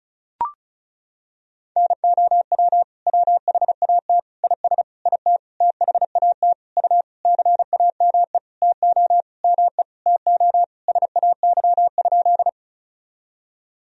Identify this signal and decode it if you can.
Morse code, heard as a keyed tone, about 35 wpm, 700 Hz. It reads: NOW WHAT IS IT THAT U CAME TO ME TO SAY?
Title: Sentences from Top 200 Words and Abbreviations 35wpm